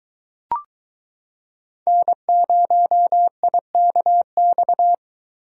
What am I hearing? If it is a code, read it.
Morse code, heard as a keyed tone, about 22 wpm, 700 Hz. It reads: N0IKX